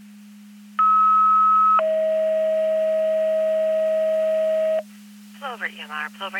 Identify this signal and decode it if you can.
Two-tone page, heard as a one-second tone, then a three-second tone, about 1290 then 640 Hz, with speech. It is heard from 0.8 to 4.8 s.